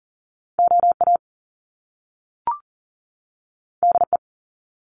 Morse code, heard as a keyed tone, about 40 wpm, 700 Hz. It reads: OA DE
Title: Top Two Letter Combinations in Words 40wpm